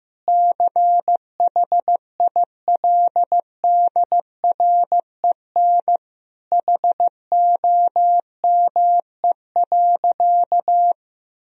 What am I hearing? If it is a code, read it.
Morse code, heard as a keyed tone, about 15 wpm, 700 Hz. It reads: CHILDREN HOME.
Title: Sentences from Top 1000 Words - Review Part 3 - 15wpm